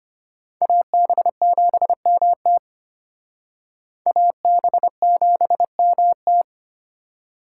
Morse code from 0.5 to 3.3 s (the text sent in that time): AB7MT